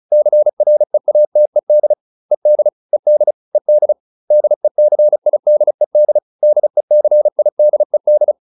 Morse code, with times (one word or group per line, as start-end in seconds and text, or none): 0.1-1.9 CREATED
2.3-2.7 ED
2.9-3.3 ED
3.5-3.9 ED
4.3-6.2 DECIDED
6.4-8.3 DECIDED